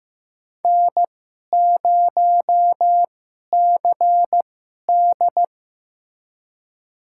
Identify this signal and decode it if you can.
Morse code, heard as a keyed tone, about 15 wpm, spaced as for 11 wpm, 700 Hz. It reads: N0CD
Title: USA Call Signs - 4 Characters - 2x Character Spacing - 15wpm